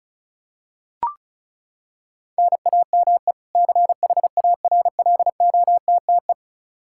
Morse code, heard as a keyed tone, about 36 wpm, 700 Hz. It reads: NAME CHARLOTTE
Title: Practice QSO - Exchange 1 - Beginning - 35wpm